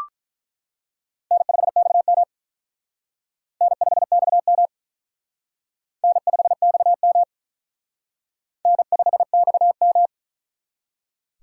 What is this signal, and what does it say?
Morse code, heard as a keyed tone, about 42 wpm, 700 Hz. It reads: N5XM N5XM N5XM N5XM
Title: Speed Racing - USA 1x2 Call Signs 35wpm